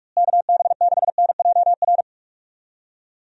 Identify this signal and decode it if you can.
Morse code, heard as a keyed tone, about 46 wpm, 700 Hz. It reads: KB6NJR
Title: USA Call Signs - 6 Characters - Repeated 3x - 45wpm